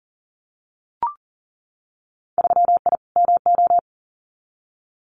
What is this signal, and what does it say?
Morse code, heard as a keyed tone, about 42 wpm, 700 Hz. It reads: 3I MO
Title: Winter Field Day Exchanges 40wpm